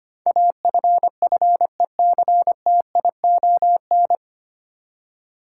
Morse code, heard as a keyed tone, about 25 wpm, 700 Hz. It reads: AFFECTION
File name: Nine Letter Words - 25wpm